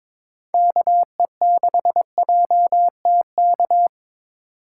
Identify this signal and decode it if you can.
Morse code, heard as a keyed tone, about 22 wpm, 700 Hz. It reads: KE6JTK
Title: USA Call Signs - 4x Character Spacing - 22wpm